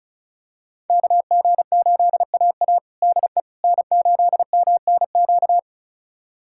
Morse code, heard as a keyed tone, about 36 wpm, 700 Hz. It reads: KG8AA DE N8MNQ
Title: USA Callsign DE Callsign - 5 Characters - 35wpm